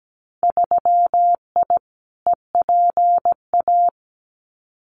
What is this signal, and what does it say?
Morse code, heard as a keyed tone, about 17 wpm, 700 Hz. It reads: 3I EPA
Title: Winter Field Day Exchanges Encore 17wpm